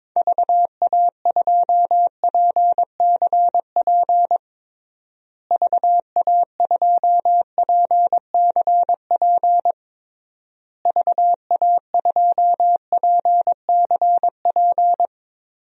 Morse code, heard as a keyed tone, about 22 wpm, 700 Hz. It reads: VA2PCP VA2PCP VA2PCP
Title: DXCC-100 Call Signs - 6 Characters - Repeated 3x - 22wpm